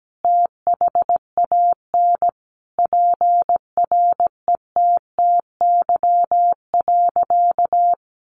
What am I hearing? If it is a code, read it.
Morse code, heard as a keyed tone, about 17 wpm, 700 Hz. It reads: THAN PRETTY.
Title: Sentences from Top 1000 Words 17wpm